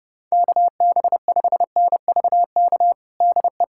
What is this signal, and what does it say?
Morse code, heard as a keyed tone, about 30 wpm, 700 Hz. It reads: KB5NVK DE
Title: USA Callsign DE Callsign 30wpm